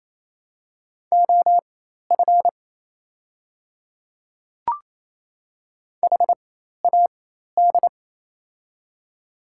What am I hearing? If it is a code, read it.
Morse code, heard as a keyed tone, about 27 wpm, spaced as for 13 wpm, 690 Hz. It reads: OF HAD
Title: 4x Character Spacing - Top 200 Words - 28wpm